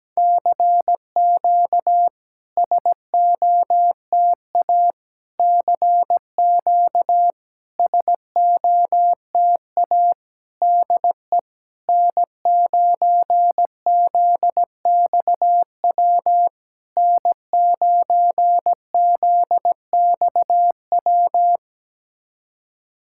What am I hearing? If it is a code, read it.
Morse code, heard as a keyed tone, about 17 wpm, 660 Hz. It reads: CQ SOTA CQ SOTA DE N9ZXW N9ZXW
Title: SOTA QSO Practice 17wpm